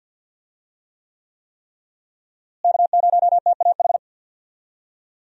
Morse code, heard as a keyed tone, about 52 wpm, 700 Hz. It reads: K0TAH
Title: USA Call Signs - 5 Characters - Repeated 3x - 50wpm